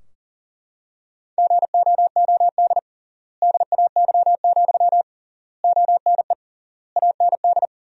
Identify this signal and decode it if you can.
Morse code, heard as a keyed tone, about 40 wpm, 700 Hz. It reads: GOOD DAY, ONE AND
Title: 3x Word-Spacing - Sentences from Top 200 Words - 40wpm